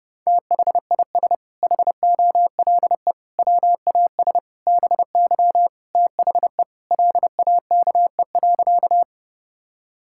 Morse code, heard as a keyed tone, about 30 wpm, 700 Hz. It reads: THIS HOLE WAS BY THE LAKE.
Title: Sentences from Top 1600 Words 30wpm